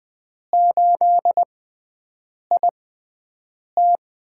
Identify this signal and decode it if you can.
Morse code, heard as a keyed tone, about 20 wpm, spaced as for 7 wpm, 700 Hz. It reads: 8IT
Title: USA Call Signs - 5 Characters - 6x Character Spacing - 20wpm